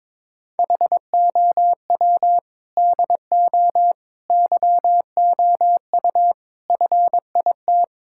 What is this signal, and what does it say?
Morse code, heard as a keyed tone, about 22 wpm, 700 Hz. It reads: HOW DO YOU FIT
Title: Sentences from Top 1000 Words - Review Part 6 - 22wpm